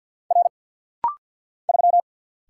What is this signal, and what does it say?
Morse code, heard as a keyed tone, about 48 wpm, 700 Hz. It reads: R 3
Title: Single Letter-Number - Rapid-Fire 50wpm